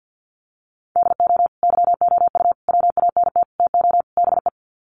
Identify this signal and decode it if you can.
Morse code, heard as a keyed tone, about 50 wpm, 700 Hz. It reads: DO YOU WANT TO BE
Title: Sets of 5 Words Encore2 50wpm